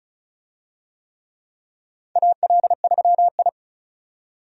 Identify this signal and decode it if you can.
Morse code, heard as a keyed tone, about 36 wpm, 700 Hz. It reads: AL3I
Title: USA Call Signs - 4 Characters - 35wpm